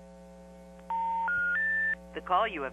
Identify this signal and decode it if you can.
Special information tone.
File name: TurtleCreekEWSD